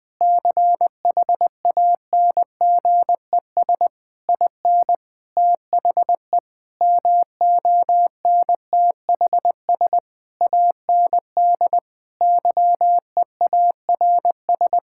Morse code, heard as a keyed tone, about 21 wpm, 700 Hz. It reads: CHANGES IN THE MONTHS AND YEARS
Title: Sentences with 7-Word Phrases - Spoken Beginning - 20wpm